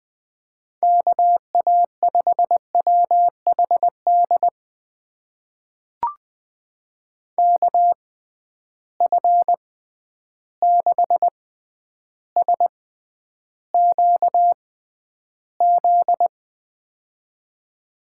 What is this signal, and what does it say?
Morse code, heard as a keyed tone, about 20 wpm, spaced as for 10 wpm, 700 Hz. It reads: KA5WHD KF6SQZ
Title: USA Call Signs - 6 Characters - 6x Character Spacing - 20wpm